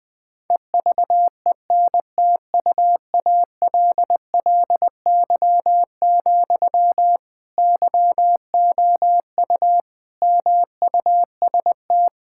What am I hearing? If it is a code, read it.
Morse code, heard as a keyed tone, about 21 wpm, 700 Hz. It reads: EVENTUALLY, YOU MUST
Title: Sentences from Top 1000 Words - Review Part 6 - 20wpm